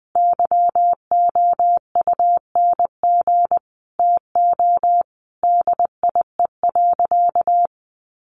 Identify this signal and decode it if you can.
Morse code, heard as a keyed tone, about 20 wpm, 700 Hz. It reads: YOUNG TO DIE.
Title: Sentences from Top 1300 Words 20wpm